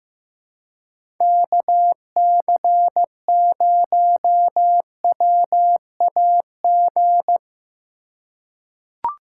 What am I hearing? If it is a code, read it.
Morse code, heard as a keyed tone, about 15 wpm, 700 Hz. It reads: KC0WAG
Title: USA Call Signs - 6 Characters - 4x Character Spacing - 15wpm